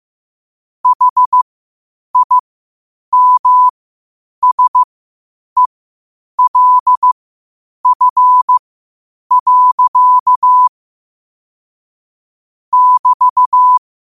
Morse code, heard as a keyed tone, about 15 wpm, spaced as for 8 wpm, 1000 Hz. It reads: HIMSELF. <BT>
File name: qotdfiles / 8wpm